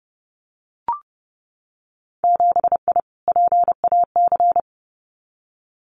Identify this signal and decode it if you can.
Morse code, heard as a keyed tone, about 30 wpm, 700 Hz. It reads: 7I PAC
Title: Winter Field Day Exchanges Encore 30wpm